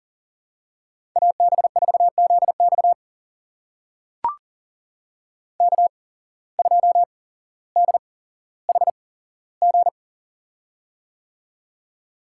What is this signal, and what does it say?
Morse code, heard as a keyed tone, about 39 wpm, spaced as for 17 wpm, 700 Hz. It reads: AB4ZX K2DHG
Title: USA Call Signs - 5 Characters - 8x Character Spacing - 40wpm